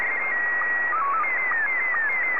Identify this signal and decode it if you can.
SSTV (slow-scan television), Scottie 1.